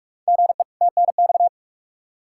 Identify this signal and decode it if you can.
Morse code, heard as a keyed tone, about 45 wpm, 700 Hz. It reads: GE TNX
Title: Practice QSO - Exchange 1 - Beginning - 45wpm